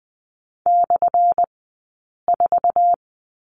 Morse code, heard as a keyed tone, about 20 wpm, spaced as for 8 wpm, 700 Hz. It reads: /4
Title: 211 Lesson 34d - Number 8 - Two Characters 20wpm